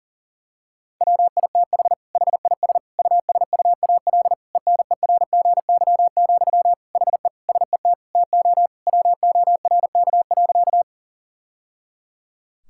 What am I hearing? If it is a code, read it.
Morse code, heard as a keyed tone, about 40 wpm, 700 Hz. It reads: WITH HIS USUAL ENERGY, HE SET TO WORK.
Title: Sentences from Top 1000 Words - Review Part 6 - 40wpm